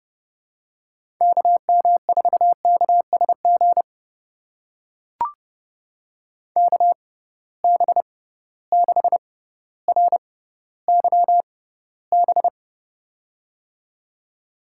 Morse code, heard as a keyed tone, about 30 wpm, spaced as for 15 wpm, 700 Hz. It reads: KM4KSG KB6RYB
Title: USA Call Signs - 6 Characters - 6x Character Spacing - 30wpm